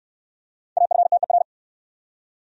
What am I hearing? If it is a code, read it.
Morse code, heard as a keyed tone, about 69 wpm, 700 Hz. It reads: AFTER